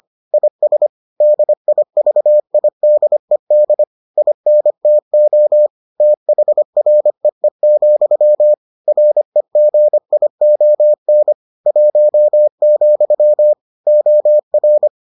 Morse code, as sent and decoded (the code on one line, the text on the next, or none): .. ... / -.. .. ...- .. -.. -.. / .. -. - --- / - .... .-. . --..-- / .-. --. .. --- -. / .---- --..-- / --- .-.
IS DIVIDED INTO THREE, REGION 1, OR